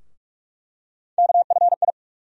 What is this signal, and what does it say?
Morse code, heard as a keyed tone, about 43 wpm, 700 Hz. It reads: KFI
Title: USA Call Sign Postfixes - 3 Letters 45wpm